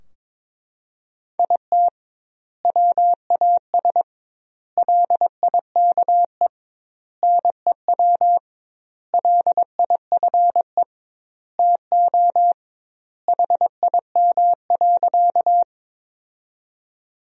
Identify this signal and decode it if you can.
Morse code, heard as a keyed tone, about 22 wpm, 700 Hz. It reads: IT WAS LIKE NEW LIFE TO HIM.
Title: 2x Word-Spacing - Sentences from Top 300 Words - 22wpm